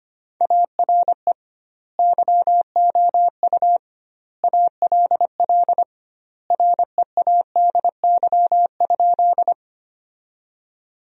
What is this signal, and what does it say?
Morse code, heard as a keyed tone, about 25 wpm, 700 Hz. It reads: ARE YOU ALL READY?
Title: 2x Word-Spacing - Sentences from Top 400 Words - 25wpm